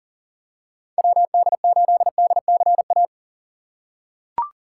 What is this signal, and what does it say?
Morse code, heard as a keyed tone, about 40 wpm, 700 Hz. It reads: WD8DCA
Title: USA Call Signs - 6 Characters - 3x Character Spacing - 40wpm